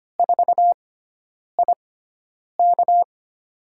Morse code, heard as a keyed tone, about 25 wpm, spaced as for 9 wpm, 700 Hz. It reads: 4IK
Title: USA Call Signs - 5 Characters - 6x Character Spacing - 25wpm